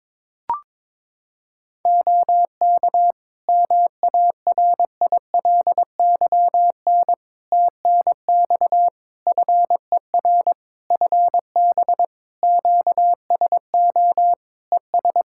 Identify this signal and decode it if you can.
Morse code, heard as a keyed tone, about 22 wpm, 700 Hz. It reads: OK MARILYN TNX FER FB QSO ES
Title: Practice QSO - Exchange 4 - Ending - 22wpm